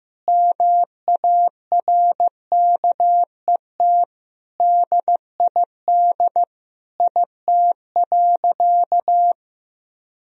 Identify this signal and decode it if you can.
Morse code, heard as a keyed tone, about 14 wpm, 700 Hz. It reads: MARKET DID IT.
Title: Sentences from Top 1000 Words - Review Part 3 - 15wpm